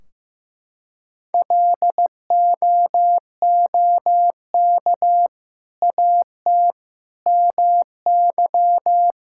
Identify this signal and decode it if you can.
Morse code, heard as a keyed tone, about 16 wpm, 700 Hz. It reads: LOOK AT MY
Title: Sentences from Top 1600 Words 15wpm